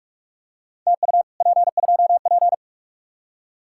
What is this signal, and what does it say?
Morse code, heard as a keyed tone, about 44 wpm, 700 Hz. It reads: TU P2P